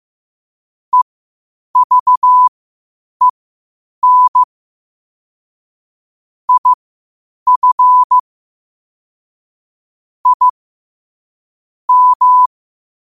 Morse code, heard as a keyed tone, about 15 wpm, spaced as for 8 wpm, 1000 Hz. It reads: EVEN IF I M
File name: qotdfiles / 8wpm